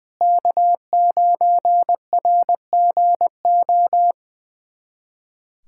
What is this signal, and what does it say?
Morse code, heard as a keyed tone, about 20 wpm, 700 Hz. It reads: K9RGO
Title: USA Callsign DE Callsign - 5 Characters - 20wpm